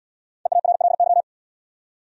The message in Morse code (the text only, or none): EARLY